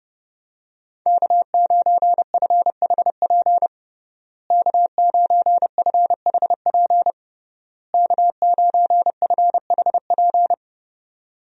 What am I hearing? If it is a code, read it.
Morse code, heard as a keyed tone, about 30 wpm, 700 Hz. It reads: K9FHP K9FHP K9FHP
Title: USA Call Signs - 5 Characters - Repeated 3x - 30wpm